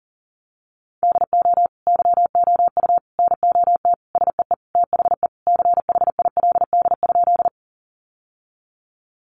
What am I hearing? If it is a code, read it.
Morse code, heard as a keyed tone, about 40 wpm, 700 Hz. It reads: DO YOU NOT SEE THE CHILD?
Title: Sentences from Top 800 Words 40wpm